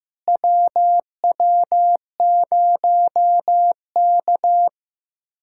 Morse code, heard as a keyed tone, about 15 wpm, 700 Hz. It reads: WW0K
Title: USA Callsign DE Callsign - 4 Characters - 15wpm